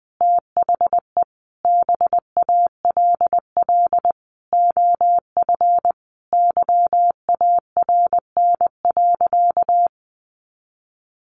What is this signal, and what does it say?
Morse code, heard as a keyed tone, about 20 wpm, 700 Hz. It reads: THE BALL OF YARN.
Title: Additional Easy Phrases 20wpm